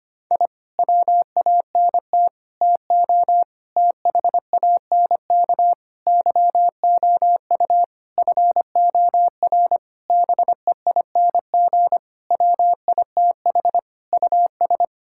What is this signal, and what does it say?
Morse code, heard as a keyed tone, about 25 wpm, 700 Hz. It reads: I WANT TO THANK YOU FOR BEING WITH US
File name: Sets of 9 Words - 25wpm